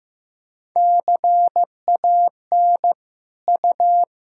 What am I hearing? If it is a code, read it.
Morse code, heard as a keyed tone, about 15 wpm, 700 Hz. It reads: CAN U
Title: Sentences from Top 200 Words and Abbreviations 15wpm